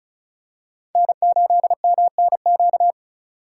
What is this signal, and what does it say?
Morse code, heard as a keyed tone, about 36 wpm, 700 Hz. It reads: N8MNQ
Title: USA Call Signs - 5 Characters - 3x Character Spacing - 35wpm